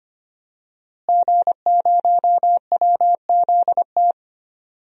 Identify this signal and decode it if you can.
Morse code, heard as a keyed tone, about 25 wpm, 700 Hz. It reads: G0WZT
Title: DXCC-100 Call Signs - 5 Characters - Repeated 2x - 25wpm